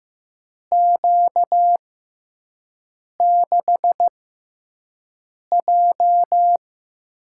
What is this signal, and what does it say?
Morse code, heard as a keyed tone, about 15 wpm, spaced as for 5 wpm, 700 Hz. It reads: Q6J
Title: USA Call Signs - 4 Characters - 6x Character Spacing - 15wpm